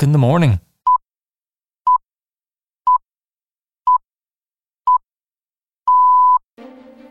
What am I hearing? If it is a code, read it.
Radio time signal (time pips), with speech. It is heard from 0.9 to 6.4 s.
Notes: 103.2 Dublin City FM